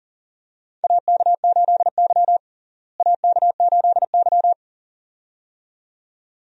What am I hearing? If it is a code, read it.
Morse code, heard as a keyed tone, about 38 wpm, 700 Hz. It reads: AK8Y AK8Y
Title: USA Call Signs - 4 Characters - Repeated 2x - 40wpm